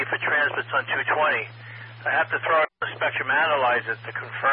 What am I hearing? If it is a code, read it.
Emergency Alert System (SAME) tones, heard as harsh buzzing bursts, with speech. It is heard from 0.2 to 4.3 s.